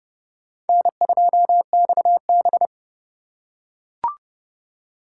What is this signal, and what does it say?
Morse code, heard as a keyed tone, about 30 wpm, 700 Hz. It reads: N2XB
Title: USA Call Signs - 3x Character Spacing - 30wpm